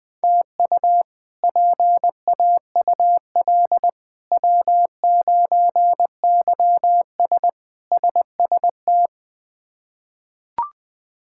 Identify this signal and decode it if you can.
Morse code, heard as a keyed tone, about 20 wpm, 700 Hz. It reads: TU PAUL W9YS SST
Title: K1USN SST QSO Practice - 20wpm